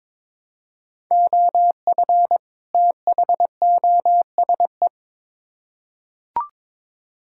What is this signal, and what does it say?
Morse code, heard as a keyed tone, about 22 wpm, 700 Hz. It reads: OF THOSE
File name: Build A Phrase - 2-Word Phrases - 22wpm